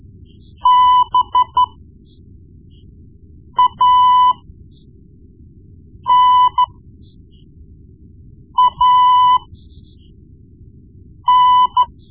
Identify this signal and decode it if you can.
Morse code, heard as a keyed tone, about 12 wpm, spaced as for 4 wpm, 1000 Hz. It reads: BANAN
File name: Morse Lessons / Morse Lessons / Tape-1B